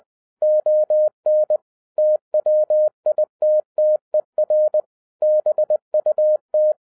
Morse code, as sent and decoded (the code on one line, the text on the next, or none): --- -. / - .-- .. - - . .-. / -... ..- -
ON TWITTER BUT